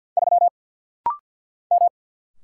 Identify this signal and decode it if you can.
Morse code, heard as a keyed tone, about 49 wpm, 700 Hz. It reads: 3 M
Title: Single Letter-Number - Rapid-Fire 50wpm